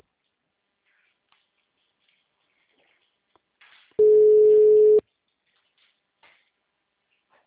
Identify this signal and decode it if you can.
Ringback tone (Europe, 425 Hz).
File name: out- -2039- - - .14932